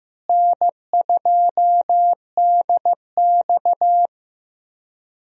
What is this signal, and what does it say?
Morse code, heard as a keyed tone, about 15 wpm, 700 Hz. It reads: N2DX
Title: USA Call Signs - 4 Characters - Repeated 3x - 15wpm